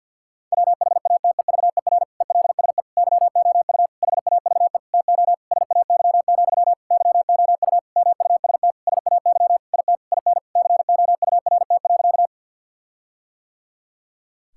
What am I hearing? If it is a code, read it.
Morse code, heard as a keyed tone, about 50 wpm, 700 Hz. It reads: WHATEVER ELSE YOU HAVE TO SAY, YOU MUST SAY IT IN COURT.
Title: Sentences from Top 1600 Words 50wpm